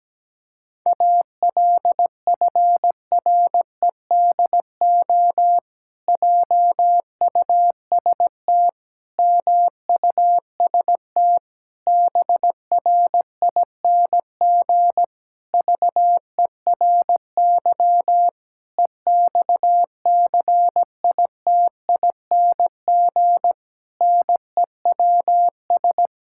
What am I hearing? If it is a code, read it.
Morse code, heard as a keyed tone, about 17 wpm, 700 Hz. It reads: ALFREDO JUST MUST BRING VERY EXCITING NEWS
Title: Pangram Challenge v2 - 17wpm